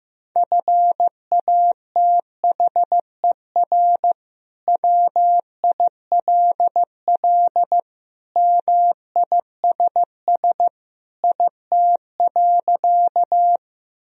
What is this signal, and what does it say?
Morse code, heard as a keyed tone, about 15 wpm, 700 Hz. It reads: FATHER WILL MISS IT.
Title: Sentences from Top 1000 Words - Review Part 3 - 15wpm